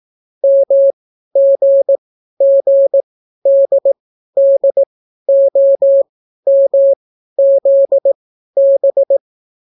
Morse code, heard as a keyed tone, about 18 wpm, spaced as for 12 wpm, 550 Hz. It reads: MGGDDOMZB